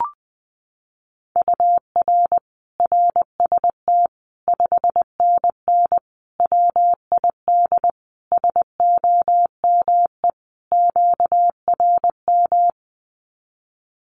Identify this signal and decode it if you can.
Morse code, heard as a keyed tone, about 20 wpm, 700 Hz. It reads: UR RST 5NN WID SOME QRM
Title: Common QSO Phrases 20wpm